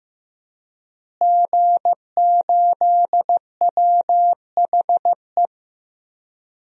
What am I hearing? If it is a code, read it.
Morse code, heard as a keyed tone, about 16 wpm, 700 Hz. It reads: G8WHE
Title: DXCC-100 Call Signs - 5 Characters - Repeated 2x - 15wpm